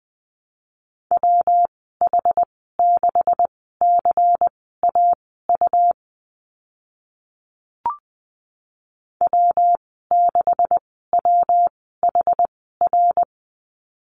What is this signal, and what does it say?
Morse code, heard as a keyed tone, about 20 wpm, spaced as for 15 wpm, 700 Hz. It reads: WH6CAU W6WHR
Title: 140 Lesson 23g - Letter f - Call Signs 20wpm